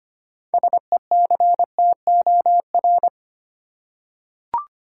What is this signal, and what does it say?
Morse code, heard as a keyed tone, about 25 wpm, 700 Hz. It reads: SECTOR
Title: Six Letter Words - 25wpm